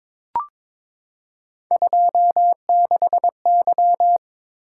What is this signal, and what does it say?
Morse code, heard as a keyed tone, about 22 wpm, 700 Hz. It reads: 26Y
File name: practice / 22wpm / Three Letter-Number Combinations - 22wpm